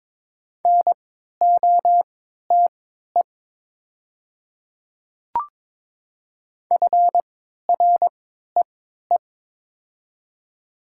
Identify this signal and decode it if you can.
Morse code, heard as a keyed tone, about 22 wpm, spaced as for 12 wpm, 690 Hz. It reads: NOTE FREE
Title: 3x Character Spacing - Top 500 Words - 22wpm